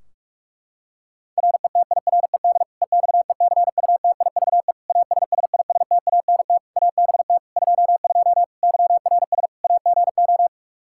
Morse code, heard as a keyed tone, about 46 wpm, 700 Hz. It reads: RETIRED EXECUTIVE ASSISTANT ABT 22 YRS AGO
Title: Practice QSO - Exchange 3 - More Info - 45wpm